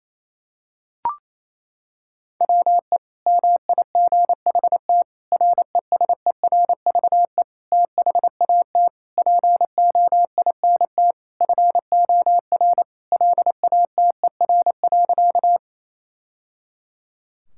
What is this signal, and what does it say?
Morse code, heard as a keyed tone, about 28 wpm, 700 Hz. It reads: WE MIGHT RESERVE THAT POINT FOR LATER.
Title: Sentences from Top 1000 Words - Review Part 6 - 28wpm